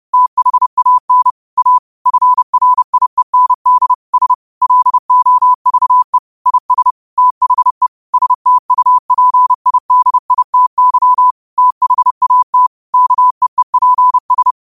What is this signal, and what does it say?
Morse code, heard as a keyed tone, about 30 wpm, 1000 Hz. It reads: THAN A FRIEND S LOVE IS THE STUPIDITY THAT KEEPS
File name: qotdfiles / 30wpm